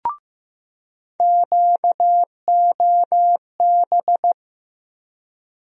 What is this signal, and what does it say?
Morse code, heard as a keyed tone, about 16 wpm, 700 Hz. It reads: QOB
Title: USA Call Sign Postfixes - 3 Letters 15wpm